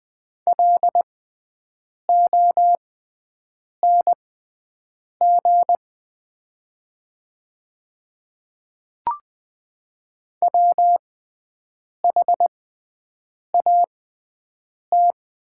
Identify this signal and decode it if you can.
Morse code, heard as a keyed tone, about 20 wpm, spaced as for 7 wpm, 690 Hz. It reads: LONG WHAT
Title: 6x Character Spacing - Top 100 Words - 20wpm